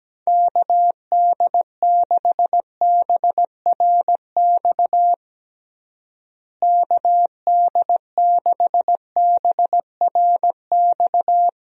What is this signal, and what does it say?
Morse code, heard as a keyed tone, about 17 wpm, 700 Hz. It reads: KD6BRX KD6BRX
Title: USA Call Signs - 6 Characters - Repeated 3x - 17wpm